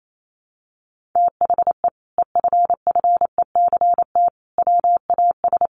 Morse code, heard as a keyed tone, about 28 wpm, 700 Hz. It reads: THE EFFECT WAS
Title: Sentences from Top 1200 Words 28wpm